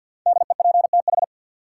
Morse code, heard as a keyed tone, about 53 wpm, 700 Hz. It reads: DEPTH
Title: Sentences from Top 1200 Words - New Word Drill - 50wpm